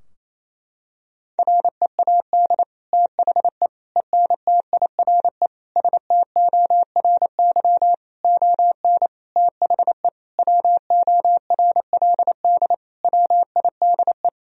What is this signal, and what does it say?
Morse code, heard as a keyed tone, about 27 wpm, 700 Hz. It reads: READ THE ENTIRE STORY ON THE WORLD WIDE
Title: Sentences from Top 1000 Words - Review Part 4 - 28wpm